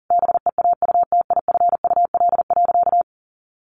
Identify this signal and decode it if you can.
Morse code, heard as a keyed tone, about 40 wpm, 700 Hz. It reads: BEAUTIFUL.
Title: Sentences from Top 1100 Words 40wpm